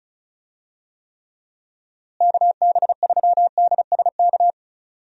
Morse code, heard as a keyed tone, about 34 wpm, 700 Hz. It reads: KB3DSK
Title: USA Call Signs - 6 Characters - Repeated 2x - 35wpm